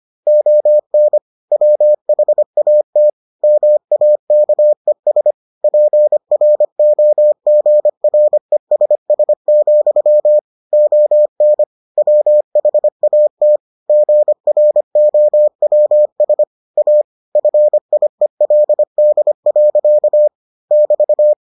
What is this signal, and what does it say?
Morse code, heard as a keyed tone, about 25 wpm, 600 Hz. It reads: ON WHAT MAKES PROGRESS, ON WHAT GROWS A FIELD. <BT>